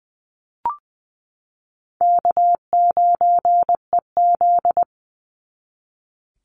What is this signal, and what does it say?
Morse code, heard as a keyed tone, about 20 wpm, 700 Hz. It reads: K9EZ
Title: NA SOTA Chaser Callsigns 20wpm